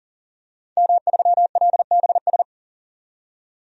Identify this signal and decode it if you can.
Morse code, heard as a keyed tone, about 37 wpm, 700 Hz. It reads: M3LBS
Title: DXCC-100 Call Signs - 5 Characters - 40wpm